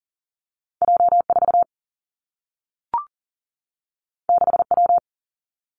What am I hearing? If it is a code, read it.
Morse code, heard as a keyed tone, about 43 wpm, 700 Hz. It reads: J 4 6 W
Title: Two Challenging Letter-Number Combinations 40wpm